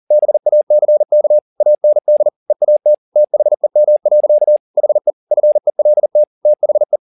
Morse code, heard as a keyed tone, about 40 wpm, 600 Hz. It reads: BACK AND EAT THEM. HE FELT THE